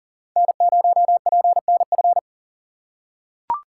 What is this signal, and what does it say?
Morse code, heard as a keyed tone, about 40 wpm, 700 Hz. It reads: N0PNF
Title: USA Call Signs - 2x Character Spacing - 40wpm